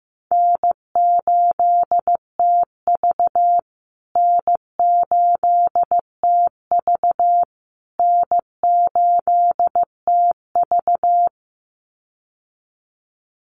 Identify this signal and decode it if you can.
Morse code, heard as a keyed tone, about 15 wpm, 700 Hz. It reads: N8TV N8TV N8TV
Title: USA Call Signs Repeated 3x 15wpm